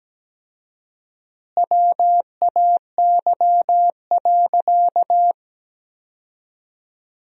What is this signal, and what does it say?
Morse code, heard as a keyed tone, about 17 wpm, 700 Hz. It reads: WAY.
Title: 4x Word-Spacing - Sentences from Top 100 Words - 17wpm